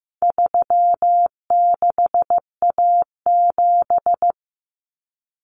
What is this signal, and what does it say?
Morse code, heard as a keyed tone, about 15 wpm, 700 Hz. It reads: 36A7
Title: Common Cut Numbers - 4-Digit 15wpm